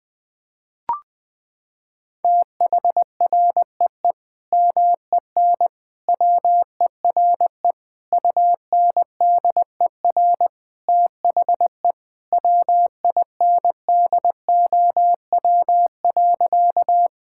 Morse code, heard as a keyed tone, about 21 wpm, 700 Hz. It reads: THREE MEN WERE UNDER THE WINDOW.
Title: Sentences from Top 1600 Words 20wpm